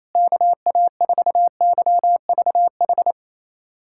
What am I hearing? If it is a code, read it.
Morse code, heard as a keyed tone, about 28 wpm, 700 Hz. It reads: KA4YVH
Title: USA Callsign DE Callsign 28wpm